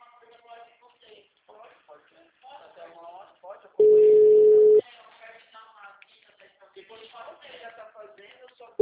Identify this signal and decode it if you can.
Ringback tone (Europe, 425 Hz), with speech.